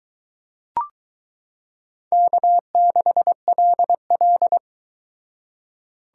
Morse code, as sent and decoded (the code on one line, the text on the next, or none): -.- -.... .-.. .-..
K6LL